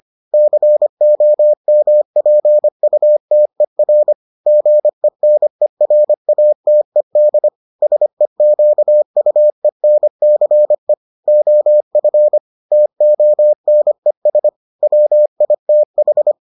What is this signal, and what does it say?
Morse code, heard as a keyed tone, about 25 wpm, 600 Hz. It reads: COMPUTER GENERATED SEQUENCE OF TONES WITH